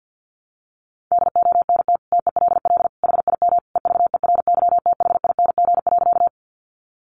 Morse code, heard as a keyed tone, about 50 wpm, 700 Hz. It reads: DONT TELL HIM EVERYTHING.
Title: Sentences from Top 700 Words 50wpm